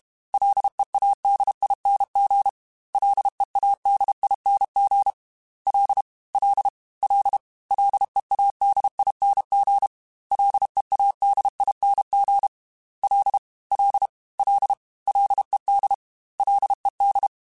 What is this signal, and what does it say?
Morse code, heard as a keyed tone, about 32 wpm, 800 Hz. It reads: LEADING LEADING L L L LEADING LEADING L L L LED LED
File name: F and L / F and L mp3 / F and L Prefixes 3x2 Twice @30